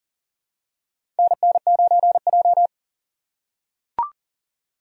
Morse code, heard as a keyed tone, about 40 wpm, 700 Hz. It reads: NN9J
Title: USA Call Signs - 4 Characters - 2x Character Spacing - 40wpm